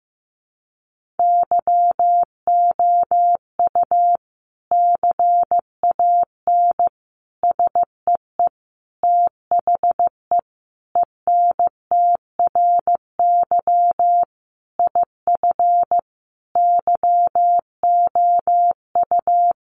Morse code, heard as a keyed tone, about 15 wpm, 700 Hz. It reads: YOU CAN SEE THE ENTRY IF YOU
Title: Sentences from Top 1500 Words 15wpm